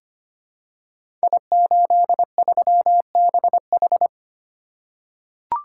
Morse code, heard as a keyed tone, about 25 wpm, 700 Hz. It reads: I83BH
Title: Code Groups - 5 Letter:Numbers - 25wpm